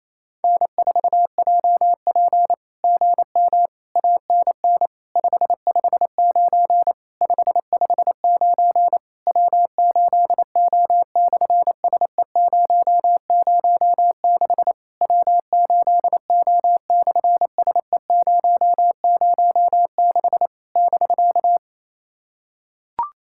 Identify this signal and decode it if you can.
Morse code, heard as a keyed tone, about 27 wpm, 700 Hz. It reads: N4JP GM ANN 559 559 W8O/SE006 W8O/SE006 <BK>
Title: SOTA QSO Practice 28wpm